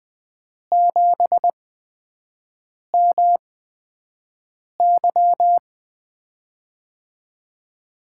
Morse code, heard as a keyed tone, about 20 wpm, 700 Hz. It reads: 7 M Y
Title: USA Call Signs - 5 Characters - 8x Character Spacing - 20wpm